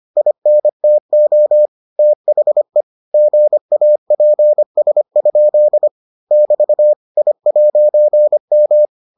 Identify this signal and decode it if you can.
Morse code, heard as a keyed tone, about 25 wpm, 600 Hz. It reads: INTO THE GAPS? <BT> I'M